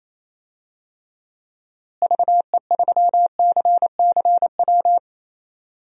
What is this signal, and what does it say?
Morse code, heard as a keyed tone, about 27 wpm, 700 Hz. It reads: VE3CCW
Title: DXCC-100 Call Signs - 6 Characters - Repeated 3x - 28wpm